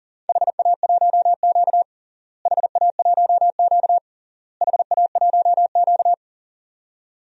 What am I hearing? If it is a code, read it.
Morse code, heard as a keyed tone, about 40 wpm, 700 Hz. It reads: HA1Q HA1Q HA1Q
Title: DXCC-100 Call Signs - 4 Characters - Repeated 3x - 40wpm